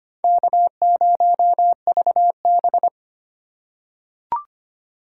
Morse code, heard as a keyed tone, about 25 wpm, 700 Hz. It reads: K0VB
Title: USA Call Signs - 4 Characters - 4x Character Spacing - 25wpm